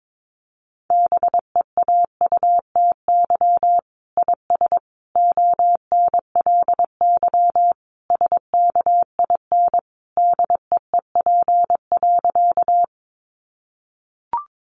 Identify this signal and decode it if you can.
Morse code, heard as a keyed tone, about 22 wpm, 700 Hz. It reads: BEAUTY IS ONLY SKIN DEEP.
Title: Sentences from Top 1300 Words 22wpm